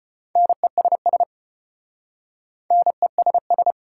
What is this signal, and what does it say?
Morse code, heard as a keyed tone, about 33 wpm, 690 Hz. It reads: NESS NESS